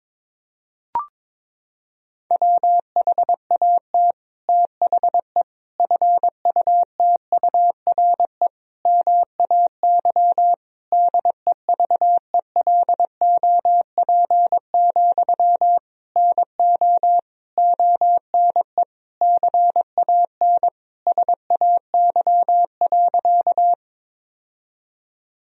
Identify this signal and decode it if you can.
Morse code, heard as a keyed tone, about 22 wpm, 700 Hz. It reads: WHAT THE FUTURE MAY DEVELOP, NO ONE CAN SAY.
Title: Sentences from Top 1000 Words - Review Part 2 - 22wpm